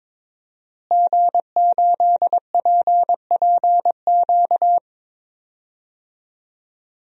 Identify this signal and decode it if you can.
Morse code, heard as a keyed tone, about 22 wpm, 700 Hz. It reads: G8PPQ
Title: DXCC-100 Call Signs - 5 Characters - Repeated 3x - 22wpm